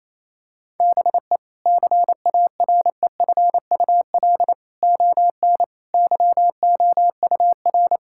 Morse code, heard as a keyed tone, about 28 wpm, 700 Hz. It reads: BE CAREFUL ON YOUR
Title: Sentences from Top 1600 Words 28wpm